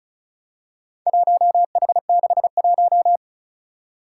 Morse code, heard as a keyed tone, about 36 wpm, 700 Hz. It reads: 1H61
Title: Four Challenging Letter-Number Combinations 35wpm